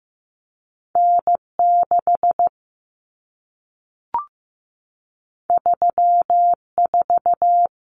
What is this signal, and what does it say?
Morse code, heard as a keyed tone, about 15 wpm, 700 Hz. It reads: N 6 3 4
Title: Common Cut Numbers - 2-Digit 15wpm